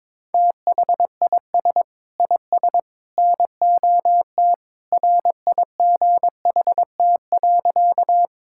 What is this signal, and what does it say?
Morse code, heard as a keyed tone, about 21 wpm, 700 Hz. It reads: THIS IS NOT RIGHT.
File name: practice / 22wpm / Sentences with 3-Word Phrases - 22wpm